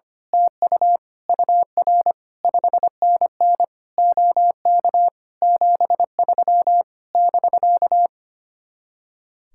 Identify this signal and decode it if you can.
Morse code, heard as a keyed tone, about 24 wpm, 700 Hz. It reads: TU UR 5NN OK 73 <BK>
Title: Speed-Racing Common QSO Phrases Encore - 25wpm